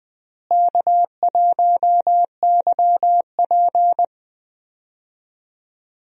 Morse code, heard as a keyed tone, about 21 wpm, 700 Hz. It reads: K1YP
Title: USA Call Signs - 4 Characters - Repeated 3x - 20wpm